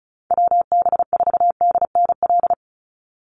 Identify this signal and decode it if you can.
Morse code, heard as a keyed tone, about 34 wpm, 700 Hz. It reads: WB4DNL
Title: Call Signs - 35wpm